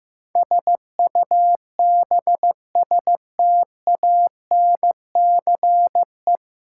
Morse code, heard as a keyed tone, about 15 wpm, 700 Hz. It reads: SUBSTANCE
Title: Sentences from Top 1500 Words - New Word Drill - 15wpm